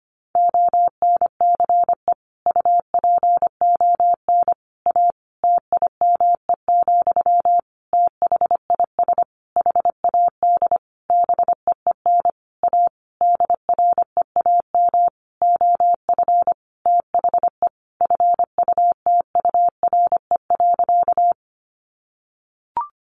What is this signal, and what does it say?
Morse code, heard as a keyed tone, about 25 wpm, 700 Hz. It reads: ONCE UPON A TIME, THIS HAD BEEN A DREAM OF THE FUTURE.
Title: Sentences from Top 1500 Words 25wpm